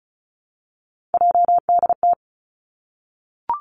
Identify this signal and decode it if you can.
Morse code, heard as a keyed tone, about 36 wpm, 700 Hz. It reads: JDT
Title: Three Challenging Letter-Number Combinations 35wpm